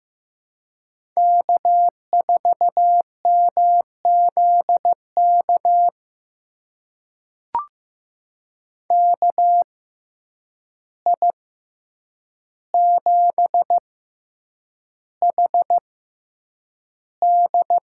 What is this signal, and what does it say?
Morse code, heard as a keyed tone, about 15 wpm, spaced as for 8 wpm, 700 Hz. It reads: K4MZK KI7HD
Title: USA Call Signs - 6x Character Spacing - 15wpm